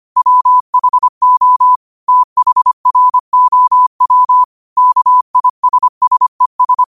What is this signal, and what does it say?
Morse code, heard as a keyed tone, about 25 wpm, 1000 Hz. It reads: WHO THROW KISSES
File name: qotdfiles / 25wpm